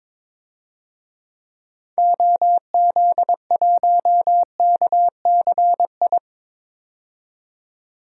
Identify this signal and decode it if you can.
Morse code, heard as a keyed tone, about 22 wpm, 700 Hz. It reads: OZ1KCI
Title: DXCC-100 Call Signs - 6 Characters - 22wpm